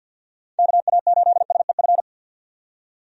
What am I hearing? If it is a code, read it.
Morse code, heard as a keyed tone, about 47 wpm, 700 Hz. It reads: KA8SEF
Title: USA Call Signs - 6 Characters - 50wpm